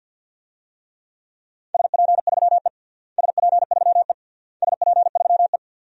Morse code, heard as a keyed tone, about 50 wpm, 700 Hz. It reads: SP3E SP3E SP3E
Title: DXCC-100 Call Signs - 4 Characters - Repeated 3x - 50wpm